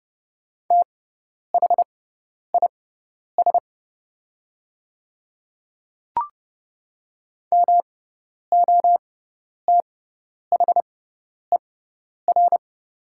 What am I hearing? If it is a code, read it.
Morse code, heard as a keyed tone, about 30 wpm, spaced as for 10 wpm, 700 Hz. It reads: THIS MOTHER
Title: 6x Character Spacing - Top 500 Words - 30wpm